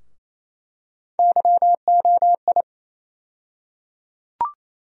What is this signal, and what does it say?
Morse code, heard as a keyed tone, about 28 wpm, 700 Hz. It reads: YOI
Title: USA Call Sign Postfixes - 3 Letters 28wpm